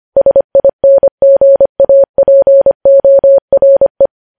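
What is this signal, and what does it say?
Morse code, heard as a keyed tone, about 25 wpm, 550 Hz. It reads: SINGAPORE
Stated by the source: News headlines sent at 25.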